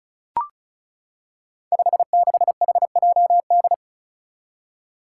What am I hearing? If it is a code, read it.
Morse code, heard as a keyed tone, about 34 wpm, 700 Hz. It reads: 56HJD
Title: Five Challenging Letter-Number Combinations 35wpm